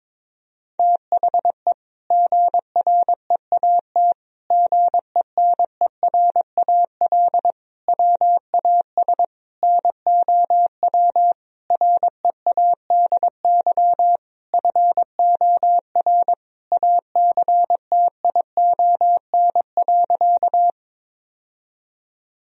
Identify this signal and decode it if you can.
Morse code, heard as a keyed tone, about 22 wpm, 700 Hz. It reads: THE GREAT GENERAL WAS NOW READY FOR ACTION.
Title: Sentences from Top 1000 Words - Review Part 4 - 22wpm